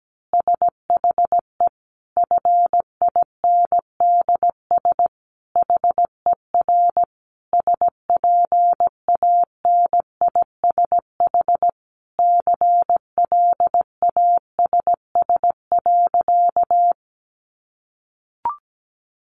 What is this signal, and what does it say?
Morse code, heard as a keyed tone, about 17 wpm, 700 Hz. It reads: SHE FINDS HER SPANISH CLASS.
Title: Additional Easy Phrases 17wpm